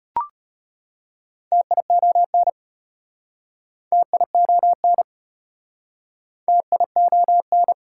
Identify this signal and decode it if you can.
Morse code, heard as a keyed tone, about 33 wpm, 700 Hz. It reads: TION TION TION